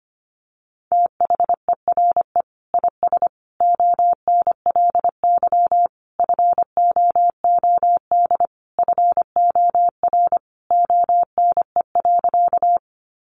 Morse code, heard as a keyed tone, about 25 wpm, 700 Hz. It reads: THERE IS ONLY FOOD FOR ONE.
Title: Sentences from Top 300 Words 25wpm